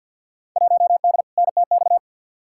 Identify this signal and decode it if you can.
Morse code, heard as a keyed tone, about 51 wpm, 700 Hz. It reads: 1D NTX
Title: Field Day Exchanges - 50wpm